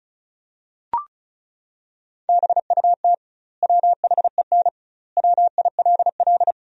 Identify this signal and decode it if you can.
Morse code, heard as a keyed tone, about 34 wpm, 700 Hz. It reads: BUT WHEN WILL
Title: 2x Word-Spacing - Sentences from Top 100 Words - 35wpm